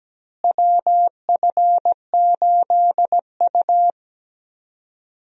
Morse code, heard as a keyed tone, about 17 wpm, 700 Hz. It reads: WF8U
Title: USA Call Signs - 4 Characters - Repeated 2x - 17wpm